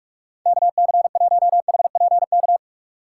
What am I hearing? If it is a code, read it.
Morse code, heard as a keyed tone, about 46 wpm, 700 Hz. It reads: KC1HPK
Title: USA Call Signs - 6 Characters - 2x Character Spacing - 45wpm